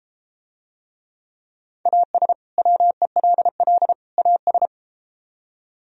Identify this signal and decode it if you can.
Morse code, heard as a keyed tone, about 32 wpm, 700 Hz. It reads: AS WELL AS